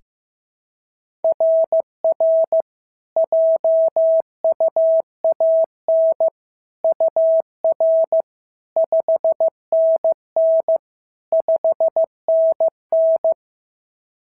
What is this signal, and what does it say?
Morse code, heard as a keyed tone, about 14 wpm, 650 Hz. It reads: RR JUAN UR 5NN 5NN
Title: SOTA QSO Practice 15wpm